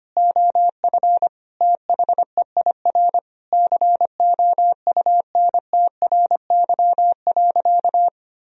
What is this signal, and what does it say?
Morse code, heard as a keyed tone, about 25 wpm, 700 Hz. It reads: OF THEIR COUNTRY.
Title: Sentences from Top 1000 Words - Review Part 1 - 25wpm